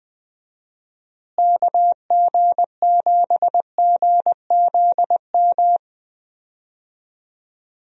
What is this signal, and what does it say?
Morse code, heard as a keyed tone, about 20 wpm, 700 Hz. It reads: KG7GZM
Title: USA Call Signs - 6 Characters - Repeated 2x - 20wpm